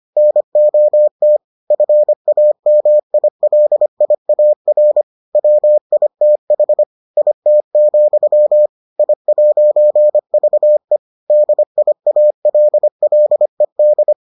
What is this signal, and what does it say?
Morse code, heard as a keyed tone, about 25 wpm, 600 Hz. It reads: NOT FAMILIAR WITH IT, I'VE DIALLED